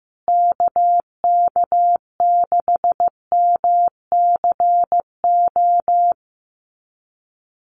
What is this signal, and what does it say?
Morse code, heard as a keyed tone, about 15 wpm, 700 Hz. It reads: KK6MCO
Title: USA Call Signs Repeated 3x 15wpm